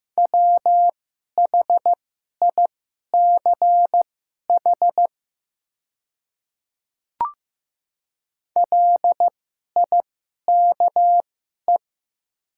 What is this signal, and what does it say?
Morse code, heard as a keyed tone, about 15 wpm, spaced as for 11 wpm, 700 Hz. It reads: WHICH LIKE
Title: 2x Character Spacing - Top 100 Words - 15wpm